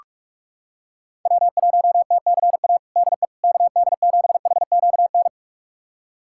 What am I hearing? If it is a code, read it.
Morse code, heard as a keyed tone, about 45 wpm, 700 Hz. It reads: W1TCA DE KD7HQN
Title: USA Callsign DE Callsign 45wpm